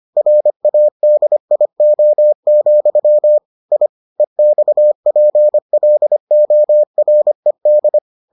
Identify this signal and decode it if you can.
Morse code, heard as a keyed tone, about 25 wpm, 600 Hz. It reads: RADIO, I EXPLORED